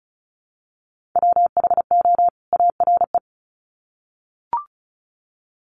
Morse code, heard as a keyed tone, about 34 wpm, 700 Hz. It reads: WHO ARE
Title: Sets of 2 Words 35wpm